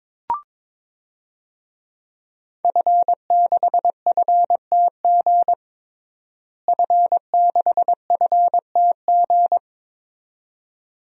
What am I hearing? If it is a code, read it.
Morse code, heard as a keyed tone, about 22 wpm, 700 Hz. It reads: F6FTG F6FTG
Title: DXCC-100 Call Signs - 5 Characters - Repeated 2x - 22wpm